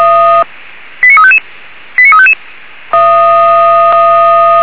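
Five-tone selective call (ZVEI), heard as quick runs of five tones, tones of about 80 ms. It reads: 89355 89355
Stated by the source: ZVEI five-tone call received on SDR.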